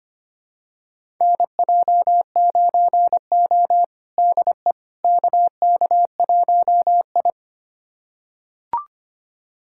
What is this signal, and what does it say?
Morse code, heard as a keyed tone, about 25 wpm, 700 Hz. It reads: NJ9O DE KK1I
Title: USA Callsign DE Callsign - 4 Characters - 25wpm